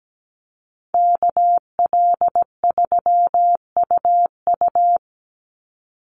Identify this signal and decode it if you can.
Morse code, heard as a keyed tone, about 17 wpm, 700 Hz. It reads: KL3UU
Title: USA Call Signs 17wpm